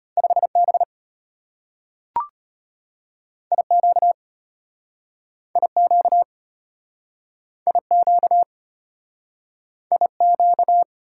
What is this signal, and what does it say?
Morse code, heard as a keyed tone, about 34 wpm, 700 Hz. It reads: 5B IQ IQ IQ IQ